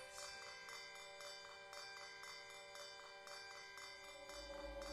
Emergency Alert System (SAME) tones, heard as harsh buzzing bursts. It is heard from 0.3 to 4.0 s.